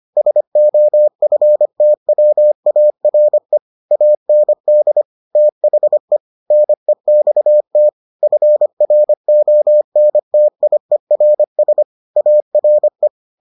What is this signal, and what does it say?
Morse code, heard as a keyed tone, about 25 wpm, 600 Hz. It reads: SOFTWARE AND THE NEXT FRONTIERS ARE